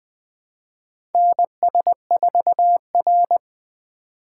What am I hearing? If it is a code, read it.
Morse code, heard as a keyed tone, about 20 wpm, 700 Hz. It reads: NS4R
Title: USA Call Signs - 4x Character Spacing - 20wpm